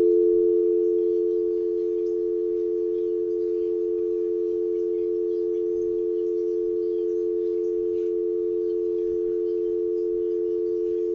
Dial tone (North America), 350 + 440 Hz.
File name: Tape038 Side2 All Trey-Mark-Phreak-Out 2005-0912